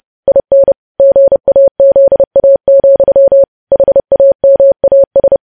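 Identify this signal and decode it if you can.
Morse code, heard as a keyed tone, about 30 wpm, 550 Hz. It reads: IN GAZA, HAMAS
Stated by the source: Morse news headlines sent at 30 wpm.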